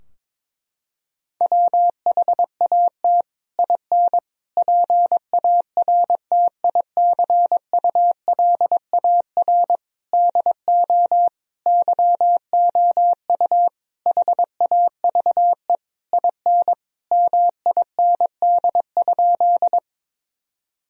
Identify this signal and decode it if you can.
Morse code, heard as a keyed tone, about 22 wpm, 700 Hz. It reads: WHAT IN PARTICULAR DO YOU HAVE IN MIND?
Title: Sentences from Top 1000 Words - Review Part 4 - 22wpm